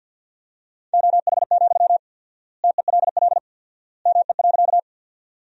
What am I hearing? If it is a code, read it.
Morse code, heard as a keyed tone, about 52 wpm, 700 Hz. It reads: OH, TELL ME.
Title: 4x Word-Spacing - Sentences from Top 500 Words - 50wpm